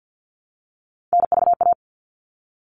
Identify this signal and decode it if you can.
Morse code, heard as a keyed tone, about 50 wpm, 700 Hz. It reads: NVA
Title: All Cut Numbers - 3-Digit 50wpm